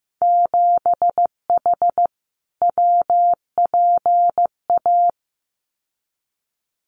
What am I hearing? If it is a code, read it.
Morse code, heard as a keyed tone, about 15 wpm, 700 Hz. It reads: 7H WPA
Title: Winter Field Day Exchanges Encore 15wpm